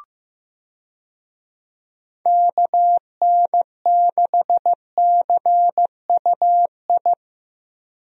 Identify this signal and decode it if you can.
Morse code, heard as a keyed tone, about 15 wpm, 700 Hz. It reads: KN6CUI
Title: USA Call Signs - 6 Characters - Repeated 3x - 15wpm